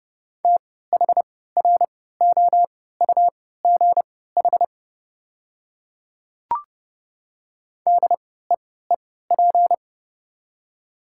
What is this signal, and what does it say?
Morse code, heard as a keyed tone, about 31 wpm, spaced as for 17 wpm, 690 Hz. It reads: THROUGH DEEP
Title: 3x Character Spacing - Top 500 Words - 30wpm